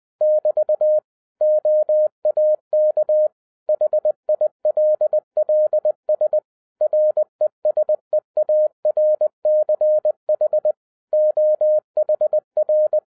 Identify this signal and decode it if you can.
Morse code, heard as a keyed tone, about 20 wpm, 610 Hz. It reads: <BT> OAK HILLS RESEARCH OHR
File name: 2011-07-mp3 / n23 QRP OTAKU 20wpm